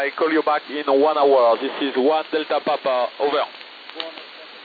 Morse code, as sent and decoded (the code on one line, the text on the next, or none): none